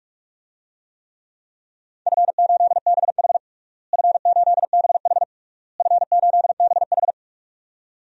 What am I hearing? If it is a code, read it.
Morse code, heard as a keyed tone, about 41 wpm, 700 Hz. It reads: F8BH F8BH F8BH
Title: DXCC-100 Call Signs - 4 Characters - Repeated 3x - 45wpm